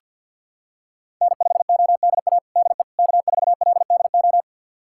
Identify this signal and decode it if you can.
Morse code, heard as a keyed tone, about 51 wpm, 700 Hz. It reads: N5ODA DE K4LDO
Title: USA Callsign DE Callsign - 5 Characters - 50wpm